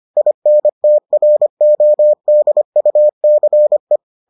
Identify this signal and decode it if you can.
Morse code, heard as a keyed tone, about 25 wpm, 600 Hz. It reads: INTRODUCE